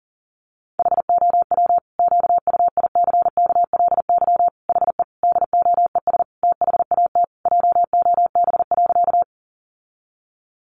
Morse code, heard as a keyed tone, about 40 wpm, 700 Hz. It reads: HOW QUICKLY HE DOES THAT JOB.
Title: Sentences from Top 700 Words 40wpm